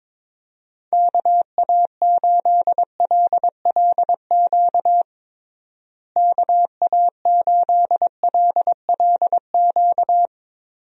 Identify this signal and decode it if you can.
Morse code, heard as a keyed tone, about 22 wpm, 700 Hz. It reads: KA8LLQ KA8LLQ
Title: USA Call Signs - 6 Characters - Repeated 3x - 22wpm